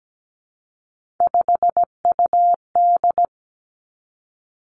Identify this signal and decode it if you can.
Morse code, heard as a keyed tone, about 17 wpm, 700 Hz. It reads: 5UD
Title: Three Challenging Letter-Number Combinations 17wpm